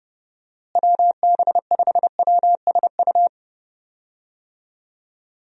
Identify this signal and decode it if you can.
Morse code, heard as a keyed tone, about 32 wpm, 700 Hz. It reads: WB5WSU
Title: USA Call Signs - 6 Characters - Repeated 2x - 30wpm